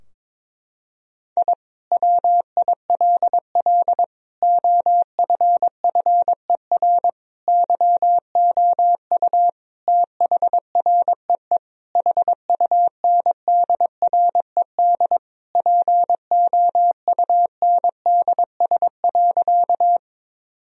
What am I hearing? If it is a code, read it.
Morse code, heard as a keyed tone, about 22 wpm, 700 Hz. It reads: I WILL OFFER YOU THREE HUNDRED POUNDS.
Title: Sentences from Top 1000 Words - Review Part 4 - 22wpm